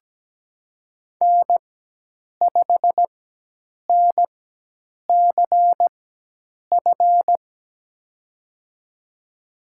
Morse code, heard as a keyed tone, about 17 wpm, spaced as for 8 wpm, 700 Hz. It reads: N5NCF
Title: USA Call Signs - 5 Characters - 4x Character Spacing - 17wpm